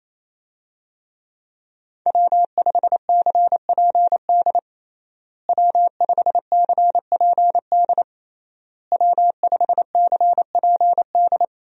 Morse code, heard as a keyed tone, about 28 wpm, 700 Hz. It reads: W5CPD W5CPD W5CPD
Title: USA Call Signs - 5 Characters - Repeated 3x - 28wpm